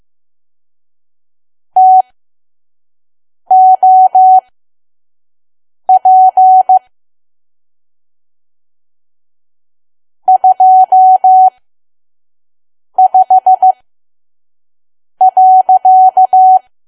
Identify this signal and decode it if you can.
Morse code, heard as a keyed tone, about 15 wpm, spaced as for 5 wpm, 750 Hz. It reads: TOP 25.
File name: Archive / 5 WPM files / PM